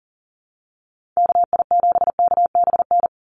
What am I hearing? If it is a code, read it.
Morse code, heard as a keyed tone, about 40 wpm, 700 Hz. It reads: KI7KBN
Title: USA Call Signs 40wpm